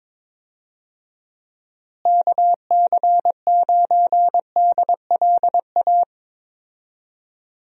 Morse code, heard as a keyed tone, about 21 wpm, 700 Hz. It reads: KC9DLA
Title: USA Call Signs - 6 Characters - Repeated 3x - 22wpm